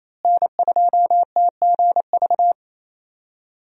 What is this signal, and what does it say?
Morse code, heard as a keyed tone, about 28 wpm, 700 Hz. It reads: N2TGV
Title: USA Call Signs - 5 Characters - Repeated 3x - 28wpm